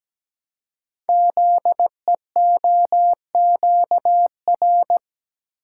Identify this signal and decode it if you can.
Morse code, heard as a keyed tone, about 17 wpm, 700 Hz. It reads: ZEOQR
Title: Code Groups - 5 Letters - 17wpm